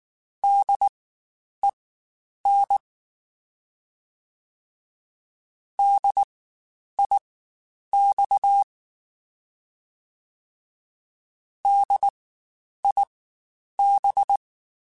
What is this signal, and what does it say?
Morse code, heard as a keyed tone, about 19 wpm, spaced as for 9 wpm, 800 Hz. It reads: DEN DIX DIB
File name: Spaced Alphas / Spaced Alphas mp3s / Ds sp @18